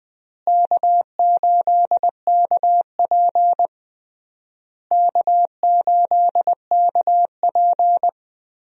Morse code, heard as a keyed tone, about 20 wpm, 700 Hz. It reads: K8KP K8KP
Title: USA Call Signs - 4 Characters - Repeated 3x - 20wpm